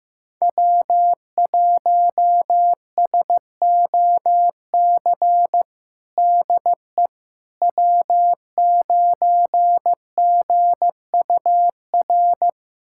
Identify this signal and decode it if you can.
Morse code, heard as a keyed tone, about 15 wpm, 700 Hz. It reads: W1SOC DE W9GUR
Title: USA Callsign DE Callsign - 5 Characters - 15wpm